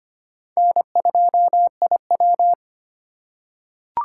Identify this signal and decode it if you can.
Morse code, heard as a keyed tone, about 25 wpm, 700 Hz. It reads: N2IW
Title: USA Call Signs - 4 Characters - 3x Character Spacing - 25wpm